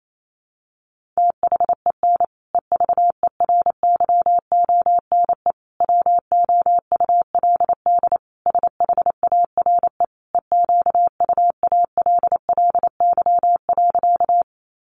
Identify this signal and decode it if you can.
Morse code, heard as a keyed tone, about 28 wpm, 700 Hz. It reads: THEN EVERYONE WOULD SHARE EQUALLY.
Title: Sentences from Top 1400 Words 28wpm